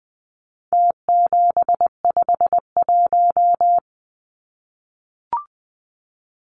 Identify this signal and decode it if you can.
Morse code, heard as a keyed tone, about 20 wpm, 700 Hz. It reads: T751
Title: Numbers - Contest Style - 4-Digit 20wpm